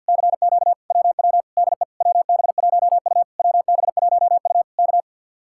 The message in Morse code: CQ WW DE WB1U WB1U K

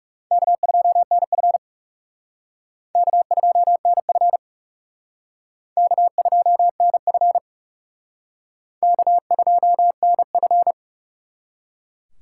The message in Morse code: K2NF K2NF K2NF K2NF

